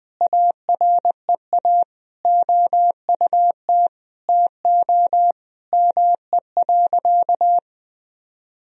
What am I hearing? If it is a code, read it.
Morse code, heard as a keyed tone, about 20 wpm, 700 Hz. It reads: AREA OUT TO ME.
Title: Sentences from Top 1000 Words - Review Part 2 - 20wpm